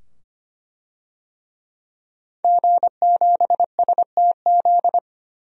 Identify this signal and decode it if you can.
Morse code, heard as a keyed tone, about 25 wpm, 700 Hz. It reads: G7STZ
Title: DXCC-100 Call Signs - 5 Characters - 25wpm